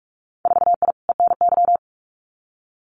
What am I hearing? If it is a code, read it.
Morse code, heard as a keyed tone, about 46 wpm, 700 Hz. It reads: 4I ENY
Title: Winter Field Day Exchanges Encore 45wpm